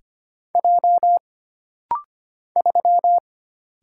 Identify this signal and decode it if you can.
Morse code, heard as a keyed tone, about 25 wpm, 700 Hz. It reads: J 3
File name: Single Character (Letter:Number:Punctuation:Prosign) - Rapid-Fire - 25wpm